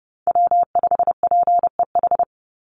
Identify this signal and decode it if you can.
Morse code, heard as a keyed tone, about 30 wpm, 700 Hz. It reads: W5PEH
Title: Call Signs Repeated 2x - 30wpm